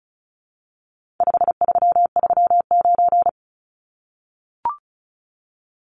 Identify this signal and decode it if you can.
Morse code, heard as a keyed tone, about 34 wpm, 700 Hz. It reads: 5339
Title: Numbers - 4-Digit 35wpm